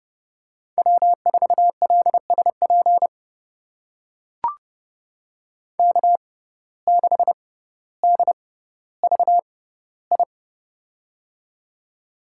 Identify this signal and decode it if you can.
Morse code, heard as a keyed tone, about 31 wpm, spaced as for 16 wpm, 700 Hz. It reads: W4LSP K6DVI
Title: USA Call Signs - 5 Characters - 6x Character Spacing - 30wpm